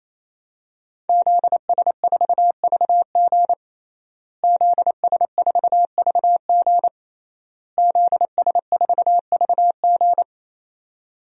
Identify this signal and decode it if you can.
Morse code, heard as a keyed tone, about 27 wpm, 700 Hz. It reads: ZS4VG ZS4VG ZS4VG
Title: DXCC-100 Call Signs - 5 Characters - Repeated 3x - 28wpm